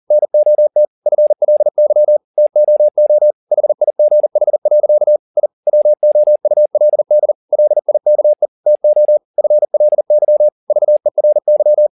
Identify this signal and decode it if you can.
Morse code, heard as a keyed tone, about 40 wpm, 600 Hz. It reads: NOT FLY TOO HIGH. I WOULD LIKE TO FLY VERY
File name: the old man and the sea 2 40wpm